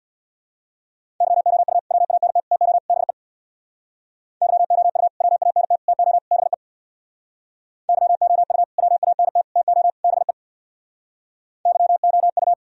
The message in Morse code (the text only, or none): YOU WANT TO BE YOU WANT TO BE YOU WANT TO BE YOU